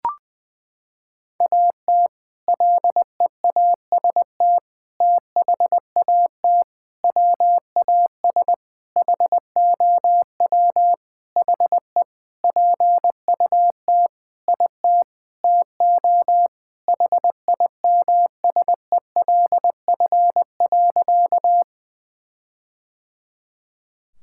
Morse code, heard as a keyed tone, about 20 wpm, 700 Hz. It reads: AT LEAST THAT WAS HOW HE PUT IT TO HIMSELF.
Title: Sentences from Top 1600 Words 20wpm